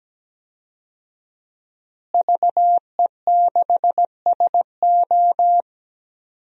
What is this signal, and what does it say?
Morse code, heard as a keyed tone, about 17 wpm, 700 Hz. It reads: VE6SO
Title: DXCC-100 Call Signs - 5 Characters - Repeated 3x - 17wpm